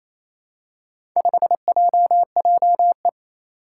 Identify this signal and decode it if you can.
Morse code, heard as a keyed tone, about 28 wpm, 700 Hz. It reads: 5JJE
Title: Four Challenging Letter-Number Combinations 28wpm